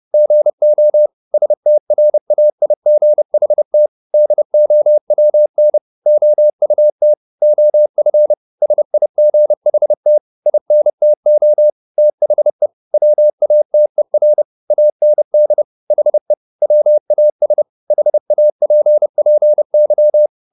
Morse code, heard as a keyed tone, about 30 wpm, 600 Hz. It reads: GO STRAIGHT DOWN OUT OF SIGHT INTO THE WATER AND HE WAS HAPPY
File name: the old man and the sea / the old man and the sea 1 30wpm